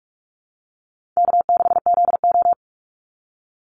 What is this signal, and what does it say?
Morse code, heard as a keyed tone, about 44 wpm, 700 Hz. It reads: K6ZO
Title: Call Signs - 45wpm